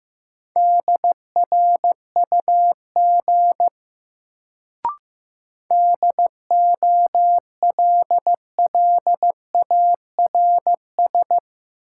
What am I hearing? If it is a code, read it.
Morse code, heard as a keyed tone, about 15 wpm, 700 Hz. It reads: DRUG DOLLARS
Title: Sets of 5 Words - New Word Drill - 15wpm